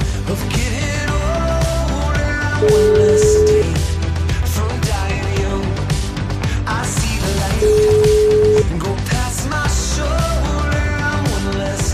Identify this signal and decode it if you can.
Ringback tone (Europe, 425 Hz), with noise.